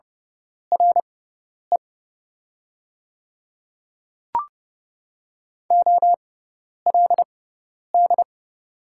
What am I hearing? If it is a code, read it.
Morse code, heard as a keyed tone, about 30 wpm, spaced as for 10 wpm, 700 Hz. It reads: RE OLD
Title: 6x Character Spacing - Top 200 Words - 30wpm